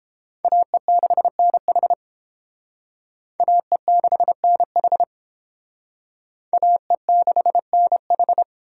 Morse code, heard as a keyed tone, about 30 wpm, 700 Hz. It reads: AE6NH AE6NH AE6NH